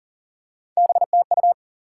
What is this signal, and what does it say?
Morse code, heard as a keyed tone, about 42 wpm, 700 Hz. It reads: BTU
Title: Top 180 QSO Elements 40wpm